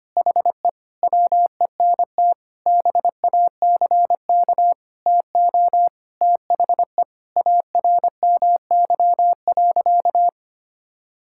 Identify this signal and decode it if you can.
Morse code, heard as a keyed tone, about 25 wpm, 700 Hz. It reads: HE WENT BACK TO THE ARMY.
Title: Sentences from Top 1000 Words - Review Part 6 - 25wpm